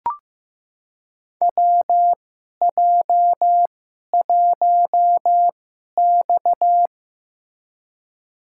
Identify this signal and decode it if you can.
Morse code, heard as a keyed tone, about 15 wpm, spaced as for 11 wpm, 700 Hz. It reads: WJ1X
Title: USA Call Signs - 4 Characters - 2x Character Spacing - 15wpm